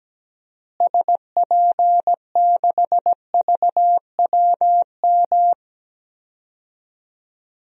Morse code, heard as a keyed tone, about 17 wpm, 700 Hz. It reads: SP6VWM
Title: DXCC-100 Call Signs - 6 Characters - Repeated 2x - 17wpm